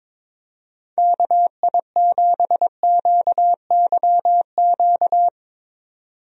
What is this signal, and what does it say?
Morse code, heard as a keyed tone, about 22 wpm, 700 Hz. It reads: KI7QYQ
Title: USA Call Signs - 6 Characters - 22wpm